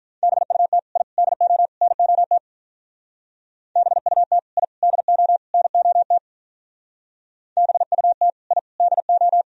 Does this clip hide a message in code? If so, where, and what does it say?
Yes: Morse code from 0.2 to 9.4 s, heard as a keyed tone, about 44 wpm, 700 Hz. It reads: BUT I DO NOT BUT I DO NOT BUT I DO